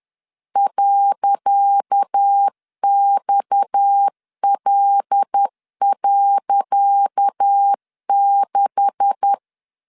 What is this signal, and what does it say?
Morse code, heard as a keyed tone, about 10 wpm, 800 Hz. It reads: .XL.6
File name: LEZIONI / 10 SIGNS / 10C DOT 10wpm 2940